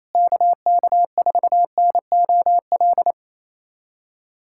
Morse code, heard as a keyed tone, about 28 wpm, 700 Hz. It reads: KK4NOL
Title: USA Callsign DE Callsign 28wpm